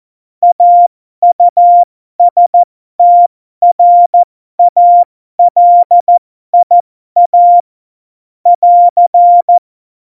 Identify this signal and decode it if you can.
Morse code, heard as a keyed tone, about 14 wpm, spaced as for 12 wpm, 700 Hz. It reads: AUSTRALIA <AR>
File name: Morsekurs / Witze